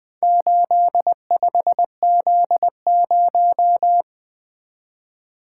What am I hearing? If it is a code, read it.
Morse code, heard as a keyed tone, about 20 wpm, 700 Hz. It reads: 85Z0
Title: Four Letter-Number Combinations - 20wpm